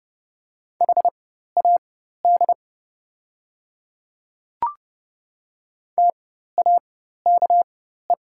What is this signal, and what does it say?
Morse code, heard as a keyed tone, about 30 wpm, spaced as for 14 wpm, 700 Hz. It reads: HAD TAKE
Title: 4x Character Spacing - Top 200 Words - 30wpm